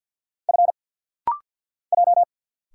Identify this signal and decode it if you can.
Morse code, heard as a keyed tone, about 50 wpm, 700 Hz. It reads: F J
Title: Single Letter-Number - Rapid-Fire 50wpm